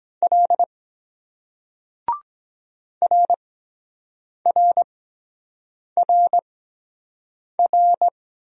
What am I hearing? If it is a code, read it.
Morse code, heard as a keyed tone, about 23 wpm, spaced as for 7 wpm, 700 Hz. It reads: L RRRR